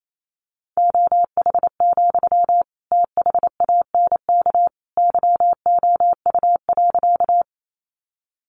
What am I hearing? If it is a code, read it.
Morse code, heard as a keyed tone, about 28 wpm, 700 Hz. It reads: OH, THANK YOU.
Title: Sentences from Top 600 Words 28wpm